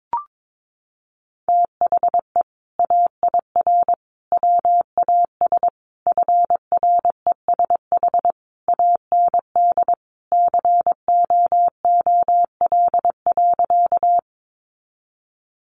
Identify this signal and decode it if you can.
Morse code, heard as a keyed tone, about 22 wpm, 700 Hz. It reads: THE AIR WAS FRESH AND COOL.
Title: Sentences from Top 1200 Words 22wpm